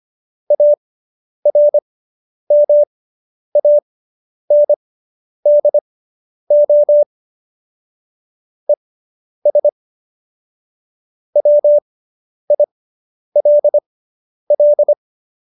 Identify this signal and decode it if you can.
Morse code, heard as a keyed tone, about 25 wpm, spaced as for 10 wpm, 590 Hz. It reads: ARMANDO ES WILL